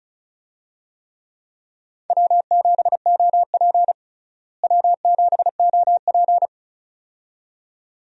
Morse code, heard as a keyed tone, about 36 wpm, 700 Hz. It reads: W7OP W7OP
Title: USA Call Signs - 4 Characters - Repeated 2x - 35wpm